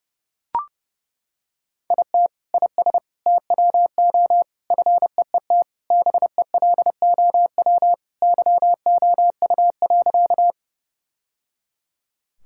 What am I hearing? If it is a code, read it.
Morse code, heard as a keyed tone, about 28 wpm, 700 Hz. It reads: IT IS TWO FEET BELOW YOU.
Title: Sentences from Top 1000 Words - Review Part 5 - 30wpm